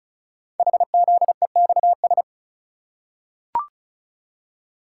Morse code, heard as a keyed tone, about 36 wpm, 700 Hz. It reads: HZEXS
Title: Code Groups - 5 Letters - 35wpm